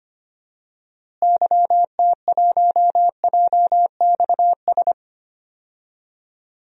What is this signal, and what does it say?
Morse code, heard as a keyed tone, about 25 wpm, 700 Hz. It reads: YT1JXS
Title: DXCC-100 Call Signs - 6 Characters - Repeated 2x - 25wpm